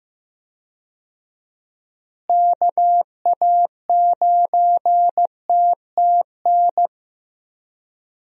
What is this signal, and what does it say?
Morse code, heard as a keyed tone, about 15 wpm, 700 Hz. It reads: KA9TTN
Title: USA Call Signs - 6 Characters - Repeated 2x - 15wpm